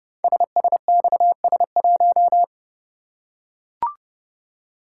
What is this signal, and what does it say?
Morse code, heard as a keyed tone, about 30 wpm, 690 Hz. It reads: SSXS1
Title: Code Groups - 5 Letter:Numbers - 30wpm